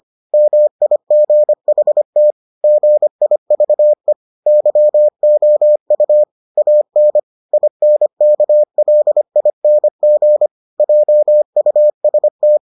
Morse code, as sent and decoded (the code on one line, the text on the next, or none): -- .. --. .... - / --. .. ...- . / -.-- --- ..- / .- -. / .. -. -.- .-.. .. -. --. / .--- ..- ... -
MIGHT GIVE YOU AN INKLING JUST